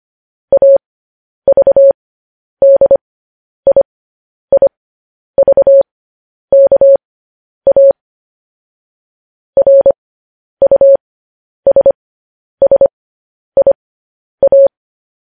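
Morse code, heard as a keyed tone, about 25 wpm, spaced as for 10 wpm, 550 Hz. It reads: AVDIIVKA RUSSIA
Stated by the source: Morse code news headlines, likely sent at 10 WPM.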